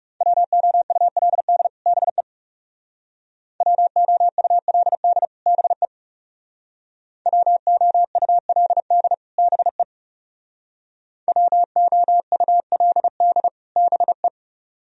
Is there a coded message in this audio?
Yes: Morse code from 0.2 to 14.3 s, heard as a keyed tone, about 39 wpm, 700 Hz. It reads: WOULD BE WOULD BE WOULD BE WOULD BE